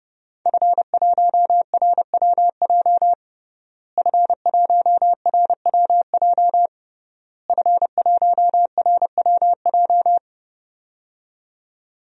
Morse code, heard as a keyed tone, about 32 wpm, 700 Hz. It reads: F1RWJ F1RWJ F1RWJ
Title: DXCC-100 Call Signs - 5 Characters - Repeated 3x - 30wpm